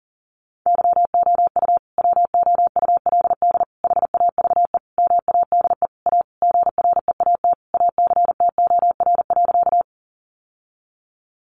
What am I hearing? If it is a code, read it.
Morse code, heard as a keyed tone, about 40 wpm, 700 Hz. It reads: YOU WOULD HAVE MADE A GREAT ACTOR.
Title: Sentences from Top 1500 Words 40wpm